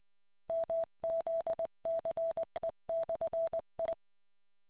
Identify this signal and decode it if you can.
Morse code, heard as a keyed tone, about 20 wpm, 660 Hz. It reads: MZCE/I